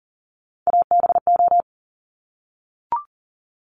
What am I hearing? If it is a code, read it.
Morse code, heard as a keyed tone, about 43 wpm, 700 Hz. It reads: ABO
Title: Top Three Letter Combinations in Words 40wpm